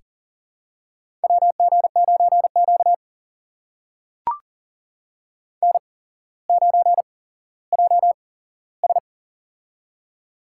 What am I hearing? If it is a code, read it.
Morse code, heard as a keyed tone, about 40 wpm, spaced as for 17 wpm, 700 Hz. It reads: WG9Q N9JS
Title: USA Call Signs - 4 Characters - 8x Character Spacing - 40wpm